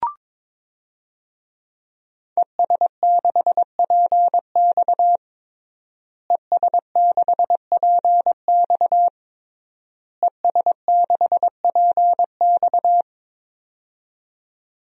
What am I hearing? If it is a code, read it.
Morse code, heard as a keyed tone, about 22 wpm, 700 Hz. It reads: ES6PX ES6PX ES6PX
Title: DXCC-100 Call Signs - 5 Characters - Repeated 3x - 22wpm